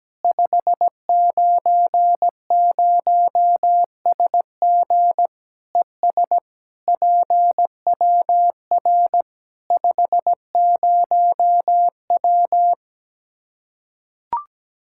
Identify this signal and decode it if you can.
Morse code, heard as a keyed tone, about 17 wpm, 700 Hz. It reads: 590SG ES PWR 50W
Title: K1USN SST QSO Practice 17wpm